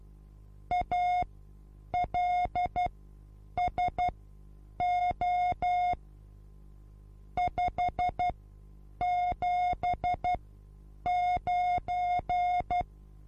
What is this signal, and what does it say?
Morse code, heard as a keyed tone, about 12 wpm, spaced as for 8 wpm, 720 Hz. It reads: ALSO 579
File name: repos / Morsekurs / DJ4UF Morsen23